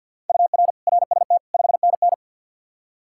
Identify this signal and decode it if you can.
Morse code, heard as a keyed tone, about 51 wpm, 700 Hz. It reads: UR RST 5NN
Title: Practice QSO - Exchange 1 - Beginning - 50wpm